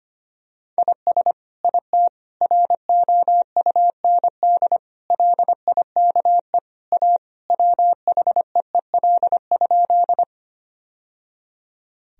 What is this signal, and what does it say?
Morse code, heard as a keyed tone, about 25 wpm, 700 Hz. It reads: IS IT ROUND LIKE A WHEEL?
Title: Sentences from Top 1000 Words - Review Part 3 - 25wpm